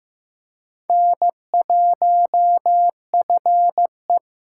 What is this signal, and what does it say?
Morse code, heard as a keyed tone, about 15 wpm, 700 Hz. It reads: N1FE
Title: K1USN SST QSO Practice - 15wpm